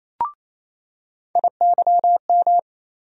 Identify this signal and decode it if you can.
Morse code, heard as a keyed tone, about 27 wpm, 700 Hz. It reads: IYM
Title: USA Call Sign Postfixes - 3 Letters 28wpm